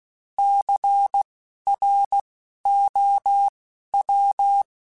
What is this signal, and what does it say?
Morse code, heard as a keyed tone, about 16 wpm, spaced as for 12 wpm, 800 Hz. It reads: CROW
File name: Alphabetical / Alphabetical mp3s / Cs @10